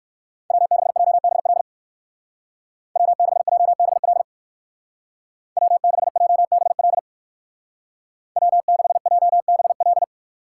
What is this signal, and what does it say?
Morse code, heard as a keyed tone, about 56 wpm, 690 Hz. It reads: W6JBL W6JBL W6JBL W6JBL
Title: Speed Racing - USA 1x3 Call Signs 45wpm